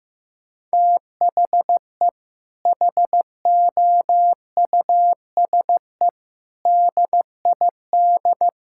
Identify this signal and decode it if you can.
Morse code, heard as a keyed tone, about 15 wpm, 700 Hz. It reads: THE HOUSE DID
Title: Sentences from Top 1600 Words 15wpm